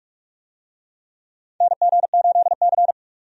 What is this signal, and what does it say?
Morse code, heard as a keyed tone, about 46 wpm, 700 Hz. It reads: NG8C
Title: USA Call Signs - 4 Characters - 45wpm